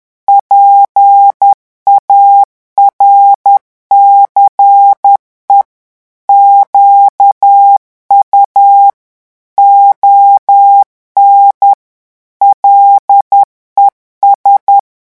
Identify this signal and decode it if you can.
Morse code, heard as a keyed tone, about 10 wpm, 800 Hz. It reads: PARCE QU ON LES
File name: revision9 citations humoristiques vitesse 10 mots minute